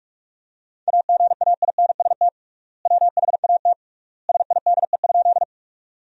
Morse code, heard as a keyed tone, about 46 wpm, 700 Hz. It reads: AGAINST WHAT SIDE?
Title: 3x Word-Spacing - Sentences from Top 400 Words - 45wpm